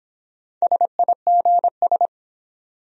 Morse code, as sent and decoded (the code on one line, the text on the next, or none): ... .. --. ...
SIGS